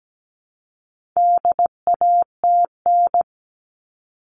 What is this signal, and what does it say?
Morse code, heard as a keyed tone, about 17 wpm, 700 Hz. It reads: DATN
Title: All Cut Numbers - 4-Digit 17wpm